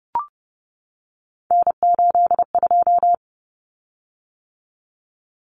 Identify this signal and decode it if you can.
Morse code, heard as a keyed tone, about 30 wpm, 700 Hz. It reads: N82
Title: Common Cut Numbers - 3-Digit 30wpm